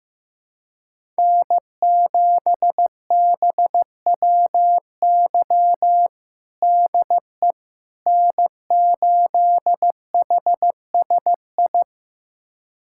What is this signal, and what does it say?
Morse code, heard as a keyed tone, about 15 wpm, 700 Hz. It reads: N7BWY DE N8HSI
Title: USA Callsign DE Callsign - 5 Characters - 15wpm